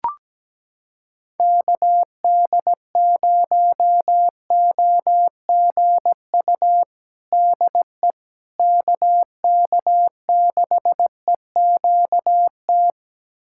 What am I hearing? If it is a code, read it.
Morse code, heard as a keyed tone, about 17 wpm, 700 Hz. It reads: KD0OGU DE KK6EQT
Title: USA Callsign DE Callsign - 6 Characters - 17wpm